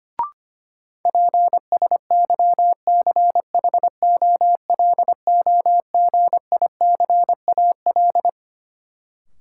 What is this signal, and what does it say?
Morse code, heard as a keyed tone, about 25 wpm, 700 Hz. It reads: PSYCHOLOGICAL
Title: Sentences from Top 1400 Words - New Word Drill - 25wpm